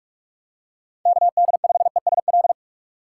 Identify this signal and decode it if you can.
Morse code, heard as a keyed tone, about 46 wpm, 700 Hz. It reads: KD5ESL
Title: USA Call Signs - 6 Characters - 2x Character Spacing - 45wpm